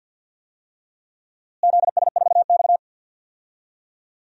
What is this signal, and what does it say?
Morse code, heard as a keyed tone, about 52 wpm, 700 Hz. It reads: ZS4X
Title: DXCC-100 Call Signs - 4 Characters - Repeated 2x - 50wpm